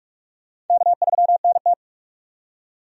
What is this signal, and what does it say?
Morse code, heard as a keyed tone, about 46 wpm, 700 Hz. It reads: K3NT
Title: USA Call Signs - 4 Characters - 45wpm